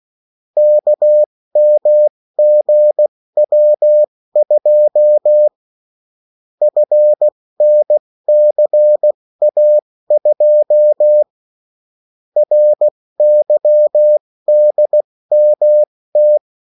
Morse code, heard as a keyed tone, about 16 wpm, spaced as for 14 wpm, 600 Hz. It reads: KMGW2 FNCA2 RYDMT
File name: morse mp3 / revision9-vitesse 14 mots